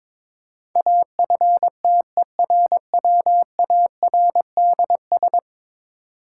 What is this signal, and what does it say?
Morse code, heard as a keyed tone, about 22 wpm, 700 Hz. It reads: AFTERWARDS
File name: practice / 22wpm / 10 Letter Words - First 3 Letters Given - 22wpm